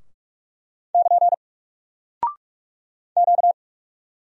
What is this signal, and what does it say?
Morse code, heard as a keyed tone, about 44 wpm, 700 Hz. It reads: <KN> Q
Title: Single Character (Letter:Number:Punctuation:Prosign) Alt 45wpm